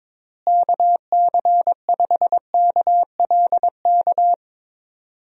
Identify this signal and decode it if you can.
Morse code, heard as a keyed tone, about 21 wpm, 700 Hz. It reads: KC5KLK
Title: USA Call Signs - 6 Characters - Repeated 2x - 22wpm